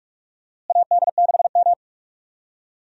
Morse code, heard as a keyed tone, about 41 wpm, 700 Hz. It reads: AD6M
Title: USA Call Signs - 4 Characters - Repeated 2x - 45wpm